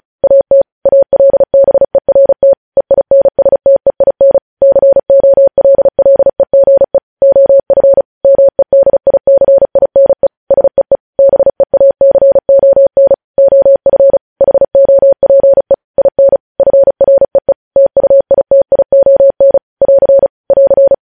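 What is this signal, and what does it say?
Morse code, heard as a keyed tone, about 36 wpm, 550 Hz. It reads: AT ALBERT EINSTEIN COLLEGE OF MEDICINE SEE BEACON OF HOPE IN FREE TUITION <AR> <AR>
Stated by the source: Morse news headlines practice at 35 words per minute.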